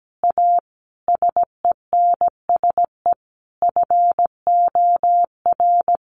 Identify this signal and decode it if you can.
Morse code, heard as a keyed tone, about 17 wpm, 700 Hz. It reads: A SENSE FOR
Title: Sentences from Top 700 Words 17wpm